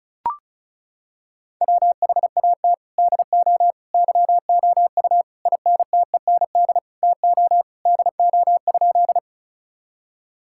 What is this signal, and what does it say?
Morse code, heard as a keyed tone, about 36 wpm, 700 Hz. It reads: WHAT DO YOU INTEND TO DO?
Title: Sentences from Top 1600 Words 35wpm